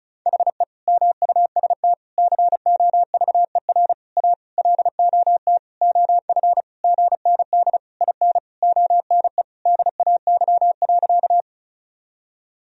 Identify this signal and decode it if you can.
Morse code, heard as a keyed tone, about 36 wpm, 700 Hz. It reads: HE MUST COVER A LOT OF GND IN ONE DAY.
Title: Sentences from Top 500 Words and Abbreviations 35wpm